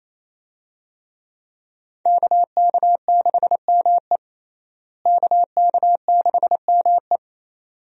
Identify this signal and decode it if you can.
Morse code, heard as a keyed tone, about 28 wpm, 700 Hz. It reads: KK6ME KK6ME
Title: USA Call Signs - 5 Characters - Repeated 2x - 28wpm